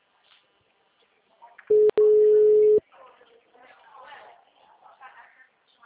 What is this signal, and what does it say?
Ringback tone (Europe, 425 Hz), with speech.